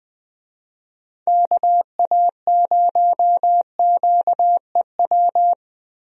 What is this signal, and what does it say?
Morse code, heard as a keyed tone, about 20 wpm, 700 Hz. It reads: KA0QEW
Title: USA Call Signs - 6 Characters - Repeated 2x - 20wpm